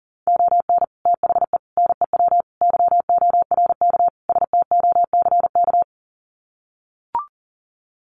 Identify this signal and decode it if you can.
Morse code, heard as a keyed tone, about 40 wpm, 700 Hz. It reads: ON THE NEW YORK STOCK
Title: Sets of 5 Words Encore 40wpm